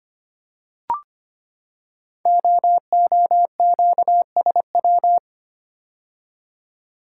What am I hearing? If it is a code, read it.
Morse code, heard as a keyed tone, about 25 wpm, 700 Hz. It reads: OOQSW
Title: Code Groups - 5 Letters - 25wpm